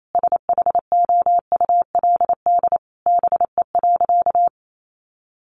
Morse code, heard as a keyed tone, about 28 wpm, 700 Hz. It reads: SHOULD BE.
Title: Sentences from Top 700 Words 28wpm